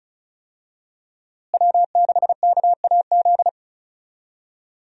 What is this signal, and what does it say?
Morse code, heard as a keyed tone, about 36 wpm, 700 Hz. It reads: W6KAZ
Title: USA Call Signs - 5 Characters - Repeated 3x - 35wpm